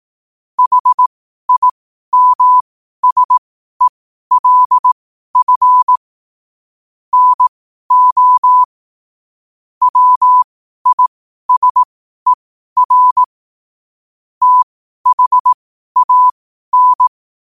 Morse code, heard as a keyed tone, about 18 wpm, spaced as for 12 wpm, 1000 Hz. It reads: HIMSELF NO WISER THAN
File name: qotdfiles / 12wpm